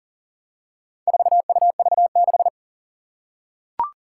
Morse code, heard as a keyed tone, about 40 wpm, 700 Hz. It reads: 4UV6
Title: Four Challenging Letter-Number Combinations 40wpm